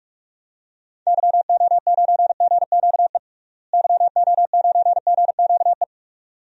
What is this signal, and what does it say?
Morse code, heard as a keyed tone, about 45 wpm, 700 Hz. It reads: YO9GQE YO9GQE
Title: DXCC-100 Call Signs - 6 Characters - Repeated 2x - 45wpm